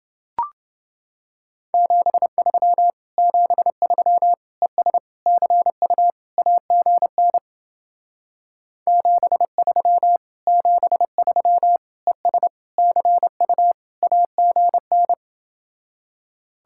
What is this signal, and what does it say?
Morse code, heard as a keyed tone, about 28 wpm, 700 Hz. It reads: 73 73 ES CU AGN 73 73 ES CU AGN